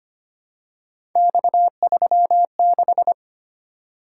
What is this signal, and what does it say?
Morse code, heard as a keyed tone, about 25 wpm, 700 Hz. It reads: X36
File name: Three Letter-Number Combinations - 25wpm